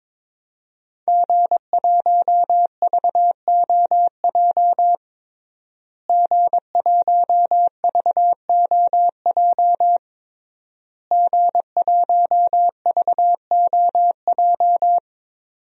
Morse code, heard as a keyed tone, about 22 wpm, 700 Hz. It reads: G1VOJ G1VOJ G1VOJ
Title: DXCC-100 Call Signs - 5 Characters - Repeated 3x - 22wpm